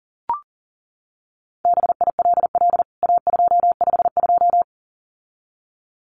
Morse code, heard as a keyed tone, about 40 wpm, 700 Hz. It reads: BILL A252
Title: CWT Exchange 40wpm